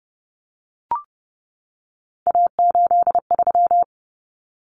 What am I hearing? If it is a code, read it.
Morse code, heard as a keyed tone, about 30 wpm, 700 Hz. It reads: A83
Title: Common Cut Numbers - 3-Digit 30wpm